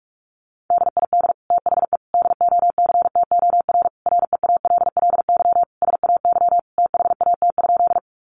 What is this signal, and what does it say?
Morse code, heard as a keyed tone, about 44 wpm, 700 Hz. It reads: DID THE DOCTOR REALLY SAY THAT?
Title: Sentences from Top 1500 Words 45wpm